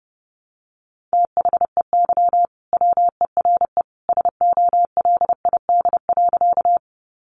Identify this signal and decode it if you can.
Morse code, heard as a keyed tone, about 30 wpm, 700 Hz. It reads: THEY WERE SOLID.
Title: Sentences from Top 1500 Words 30wpm